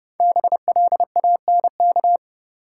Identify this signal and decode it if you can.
Morse code, heard as a keyed tone, about 30 wpm, 700 Hz. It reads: BLANK
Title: Sentences from Top 1500 Words - New Word Drill - 30wpm